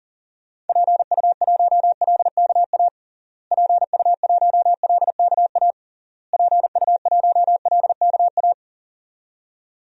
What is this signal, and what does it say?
Morse code, heard as a keyed tone, about 40 wpm, 700 Hz. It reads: PU1LKA PU1LKA PU1LKA
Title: DXCC-100 Call Signs - 6 Characters - Repeated 3x - 40wpm